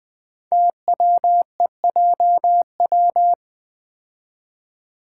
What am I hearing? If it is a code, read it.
Morse code, heard as a keyed tone, about 20 wpm, 700 Hz. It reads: TWEJW
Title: Code Groups - 5 Letters - 20wpm